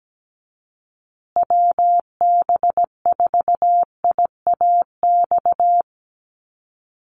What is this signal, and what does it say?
Morse code, heard as a keyed tone, about 17 wpm, 700 Hz. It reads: WB4IAX
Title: USA Call Signs Repeated 3x 17wpm